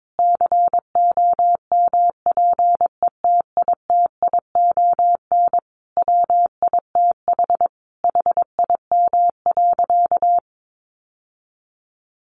Morse code, heard as a keyed tone, about 22 wpm, 700 Hz. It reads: COMPETITION WITH HIM.
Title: Sentences from Top 1000 Words 22wpm